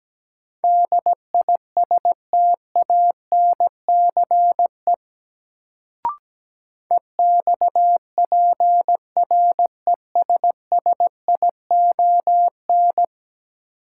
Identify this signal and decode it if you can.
Morse code, heard as a keyed tone, about 17 wpm, 700 Hz. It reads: DISTANCE EXPRESSION
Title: Sentences from Top 1200 Words - New Word Drill - 17wpm